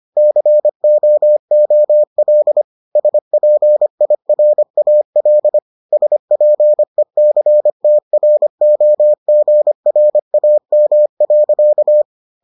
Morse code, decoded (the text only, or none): COOL SPIRAL SPECTROGRAM.